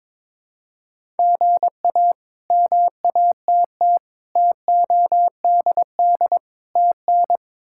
Morse code, heard as a keyed tone, about 22 wpm, 700 Hz. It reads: GA MATT TODD TN
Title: K1USN SST QSO Practice - 22wpm